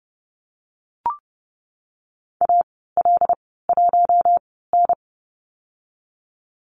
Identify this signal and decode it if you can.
Morse code, heard as a keyed tone, about 30 wpm, spaced as for 17 wpm, 700 Hz. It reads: AL1N
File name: Lesson 14g - Number 2 - Call Signs 30wpm